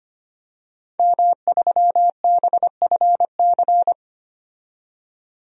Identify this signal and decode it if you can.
Morse code, heard as a keyed tone, about 25 wpm, 700 Hz. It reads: M3BFC
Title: DXCC-100 Call Signs - 5 Characters - 25wpm